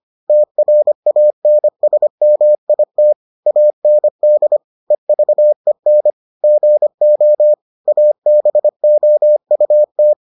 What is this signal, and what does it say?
Morse code, heard as a keyed tone, about 25 wpm, 600 Hz. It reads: TRANSMIT AND EVEN GO ABOUT